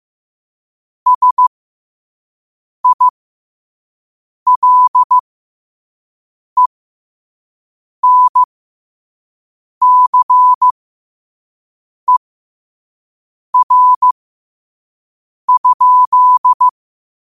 Morse code, heard as a keyed tone, about 15 wpm, spaced as for 5 wpm, 1000 Hz. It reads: SILENCER?
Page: New quotes every day in morse code at 5 Words per minute.